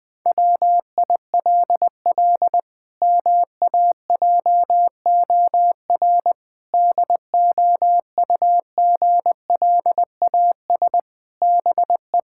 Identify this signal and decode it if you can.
Morse code, heard as a keyed tone, about 20 wpm, 700 Hz. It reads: WILL MAJOR DOUGLAS BE
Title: Pangram Challenge v2 - 20wpm